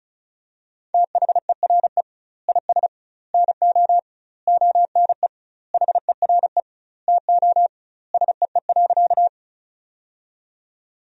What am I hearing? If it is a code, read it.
Morse code, heard as a keyed tone, about 34 wpm, 700 Hz. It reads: THERE IS NO ONE HERE TO SEE.
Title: 2x Word-Spacing - Sentences from Top 200 Words - 35wpm